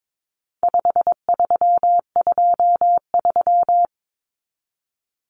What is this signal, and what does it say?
Morse code, heard as a keyed tone, about 22 wpm, 700 Hz. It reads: 5323
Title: Numbers - 4-Digit 22wpm